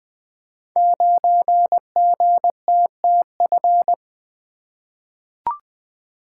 Morse code, heard as a keyed tone, about 20 wpm, 700 Hz. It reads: 9GTTF
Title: Code Groups - 5 Letter:Numbers - 20wpm